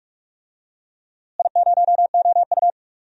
Morse code, heard as a keyed tone, about 46 wpm, 700 Hz. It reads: I0OU
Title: DXCC-100 Call Signs - 4 Characters - Repeated 2x - 45wpm